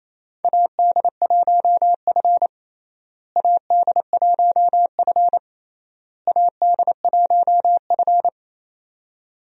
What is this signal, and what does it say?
Morse code, heard as a keyed tone, about 28 wpm, 700 Hz. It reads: AD1F AD1F AD1F
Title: USA Call Signs - 4 Characters - Repeated 3x - 28wpm